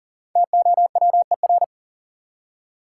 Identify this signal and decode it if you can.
Morse code, heard as a keyed tone, about 40 wpm, 700 Hz. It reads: TOWER
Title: Five Letter Words - 40wpm